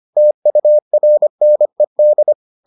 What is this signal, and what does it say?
Morse code, heard as a keyed tone, about 25 wpm, 600 Hz. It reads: TURNED